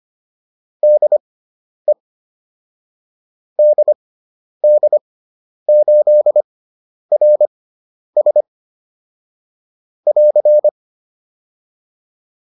Morse code, heard as a keyed tone, about 25 wpm, spaced as for 10 wpm, 600 Hz. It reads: DE DD8RS <AR>